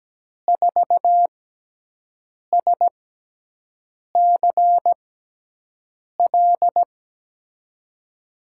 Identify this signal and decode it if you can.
Morse code, heard as a keyed tone, about 17 wpm, spaced as for 6 wpm, 700 Hz. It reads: 4SCL
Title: USA Call Signs - 6 Characters - 6x Character Spacing - 17wpm